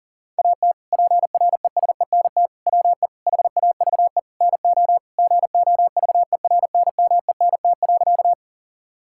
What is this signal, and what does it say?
Morse code, heard as a keyed tone, about 40 wpm, 700 Hz. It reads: AT PRESENT WE HAVE NO GOVERNMENT.
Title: Sentences from Top 1000 Words - Review Part 5 - 40wpm